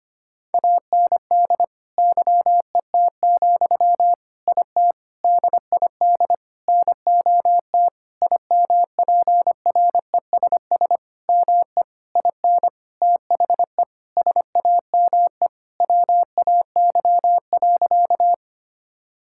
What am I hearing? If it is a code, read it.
Morse code, heard as a keyed tone, about 25 wpm, 700 Hz. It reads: AND YET, IT DID NOT IMPRESS ME IN THE SAME WAY.
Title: Sentences from Top 1600 Words 25wpm